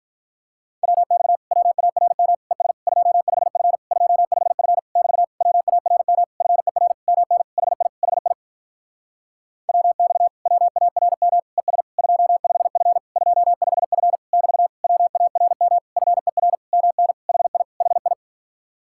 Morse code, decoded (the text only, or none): WX WARM ES 25F 25F <BT> WARM FER MN HI HI WX WARM ES 25F 25F <BT> WARM FER MN HI HI